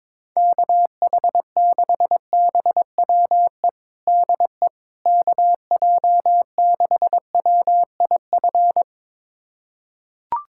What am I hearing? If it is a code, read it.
Morse code, heard as a keyed tone, about 22 wpm, 700 Hz. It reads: KH6BWE DE KJ6WIF
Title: USA Callsign DE Callsign 22wpm